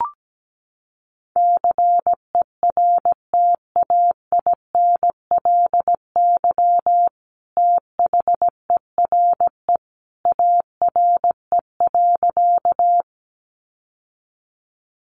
Morse code, heard as a keyed tone, about 17 wpm, 700 Hz. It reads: CERTAINLY THERE ARE.
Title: Sentences from Top 900 Words 17wpm